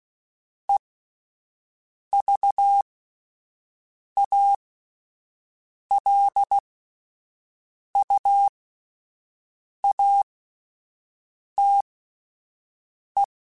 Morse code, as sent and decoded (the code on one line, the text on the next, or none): . ...- .- .-.. ..- .- - .
EVALUATE